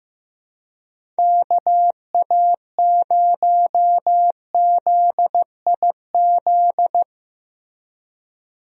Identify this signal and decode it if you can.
Morse code, heard as a keyed tone, about 15 wpm, 700 Hz. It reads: KA0ZIZ
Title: USA Call Signs - 6 Characters - Repeated 3x - 15wpm